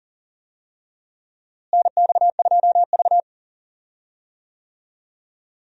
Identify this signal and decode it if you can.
Morse code, heard as a keyed tone, about 40 wpm, 700 Hz. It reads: NX2V
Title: USA Call Signs - 4 Characters - Repeated 3x - 40wpm